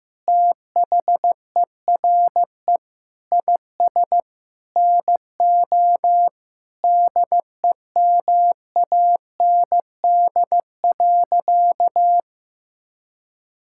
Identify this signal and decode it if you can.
Morse code, heard as a keyed tone, about 15 wpm, 700 Hz. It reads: THERE IS NO DEMAND.
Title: Sentences from Top 1000 Words - Review Part 5 - 15wpm